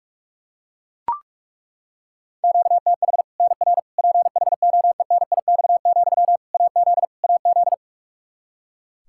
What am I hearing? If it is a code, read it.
Morse code, heard as a keyed tone, about 43 wpm, 700 Hz. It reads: QTH NR PHOENIX, AZ AZ
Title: K1USN SST QSO Practice 45wpm